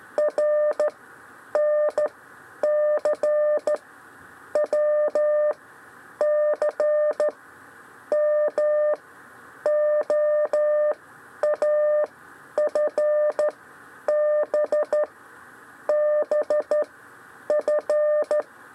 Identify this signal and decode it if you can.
Morse code, heard as a keyed tone, about 13 wpm, 600 Hz. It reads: R N C W C M O A F B B F
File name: 6 lezione cw 1